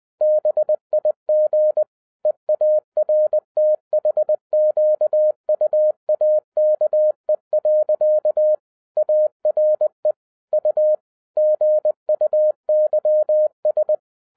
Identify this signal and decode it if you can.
Morse code, heard as a keyed tone, about 20 wpm, 610 Hz. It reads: BIG EARTHQUAKE. ARE U GUYS